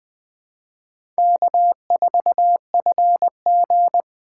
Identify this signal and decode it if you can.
Morse code, heard as a keyed tone, about 20 wpm, 700 Hz. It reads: K4FG
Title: USA Call Signs - 4 Characters - 3x Character Spacing - 20wpm